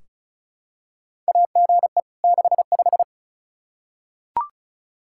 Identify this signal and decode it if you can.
Morse code, heard as a keyed tone, about 34 wpm, 700 Hz. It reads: AGE 65
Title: K1USN SST QSO Practice 35wpm